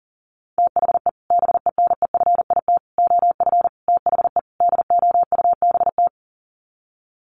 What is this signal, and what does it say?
Morse code, heard as a keyed tone, about 40 wpm, 700 Hz. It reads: THE BENEFIT OF THE DOUBT
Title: Sets of 5 Words Encore 40wpm